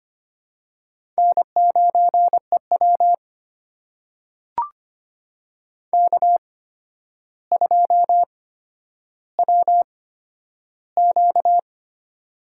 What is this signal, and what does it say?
Morse code, heard as a keyed tone, about 25 wpm, spaced as for 11 wpm, 700 Hz. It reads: N9EW K2WQ
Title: USA Call Signs - 4 Characters - 8x Character Spacing - 25wpm